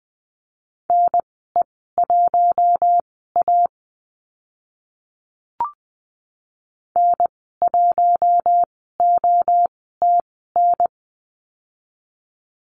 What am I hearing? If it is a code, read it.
Morse code, heard as a keyed tone, about 20 wpm, spaced as for 14 wpm, 700 Hz. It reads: NE1A N1OTN
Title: 042 Lesson 8g - Number 1 - Call Signs 20wpm